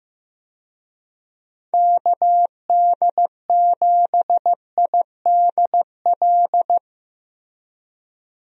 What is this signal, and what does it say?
Morse code, heard as a keyed tone, about 15 wpm, 700 Hz. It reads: KD7IDL
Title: USA Call Signs - 6 Characters - Repeated 2x - 15wpm